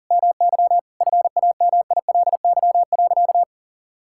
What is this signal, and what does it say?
Morse code, heard as a keyed tone, about 40 wpm, 700 Hz. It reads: MY FAMILY.
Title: Sentences from Top 1600 Words 40wpm